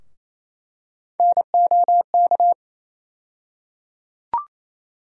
Morse code, heard as a keyed tone, about 28 wpm, 700 Hz. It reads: NOK
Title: USA Call Sign Postfixes - 3 Letters 28wpm